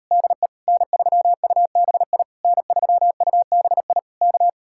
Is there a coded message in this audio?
Yes: Morse code from 0.1 to 4.5 s, heard as a keyed tone, about 38 wpm, 700 Hz. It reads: DE N3UBI N3UBI K